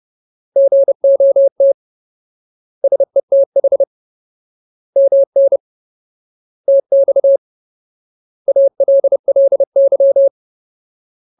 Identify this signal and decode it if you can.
Morse code, heard as a keyed tone, about 30 wpm, 550 Hz. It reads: GOT SETH MN TX ALLY
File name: Ultimate Word List / 30WPM Ultimate